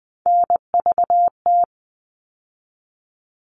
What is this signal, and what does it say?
Morse code, heard as a keyed tone, about 20 wpm, 700 Hz. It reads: NVT
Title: All Cut Numbers - 3-Digit 20wpm